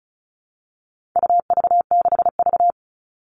Morse code, heard as a keyed tone, about 34 wpm, 700 Hz. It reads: UV6V
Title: All Cut Numbers - 4-Digit 35wpm